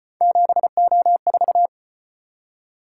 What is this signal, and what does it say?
Morse code, heard as a keyed tone, about 33 wpm, 700 Hz. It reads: 7O4